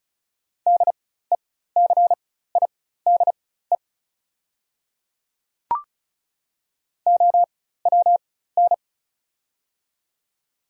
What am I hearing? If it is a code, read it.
Morse code, heard as a keyed tone, about 34 wpm, spaced as for 16 wpm, 700 Hz. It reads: DECIDE OWN
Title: 4x Character Spacing - Top 500 Words - 35wpm